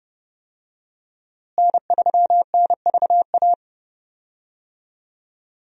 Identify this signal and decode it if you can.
Morse code, heard as a keyed tone, about 30 wpm, 700 Hz. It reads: N3NVA
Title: USA Call Signs - 5 Characters - Repeated 3x - 30wpm